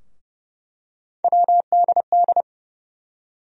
Morse code, heard as a keyed tone, about 30 wpm, 700 Hz. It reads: WDD
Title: USA Call Sign Postfixes - 3 Letters 30wpm